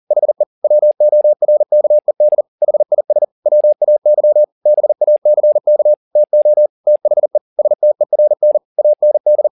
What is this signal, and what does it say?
Morse code, heard as a keyed tone, about 40 wpm, 600 Hz. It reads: HE WORKED HIS WAY BACK TO THE STERN AND
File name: the old man and the sea 2 40wpm